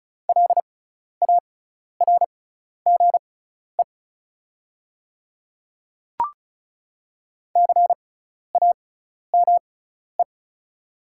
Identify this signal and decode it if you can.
Morse code, heard as a keyed tone, about 36 wpm, spaced as for 12 wpm, 690 Hz. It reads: LARGE CAME
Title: 6x Character Spacing - Top 300 Words - 35wpm